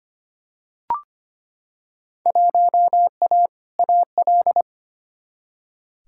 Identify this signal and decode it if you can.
Morse code, heard as a keyed tone, about 25 wpm, 700 Hz. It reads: 1A AL
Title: Field Day Exchanges - 25wpm